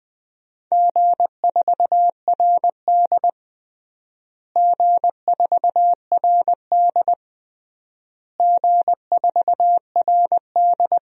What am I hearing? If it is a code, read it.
Morse code, heard as a keyed tone, about 20 wpm, 700 Hz. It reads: G4RD G4RD G4RD
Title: DXCC-100 Call Signs - 4 Characters - Repeated 3x - 20wpm